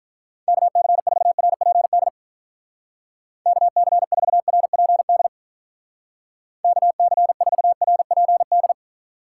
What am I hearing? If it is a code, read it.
Morse code, heard as a keyed tone, about 44 wpm, 700 Hz. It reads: KC4RPD KC4RPD KC4RPD